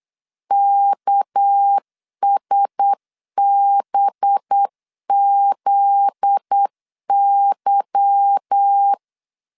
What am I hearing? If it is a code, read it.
Morse code, heard as a keyed tone, about 8 wpm, 800 Hz. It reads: KSBZY
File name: LEZIONI / 8 YJ45 / 8 Y 8wpm 3245